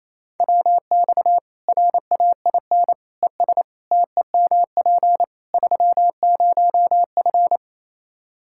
Morse code, heard as a keyed tone, about 28 wpm, 700 Hz. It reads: WX RAIN ES TEMP 30F
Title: K1USN SST QSO Practice 28wpm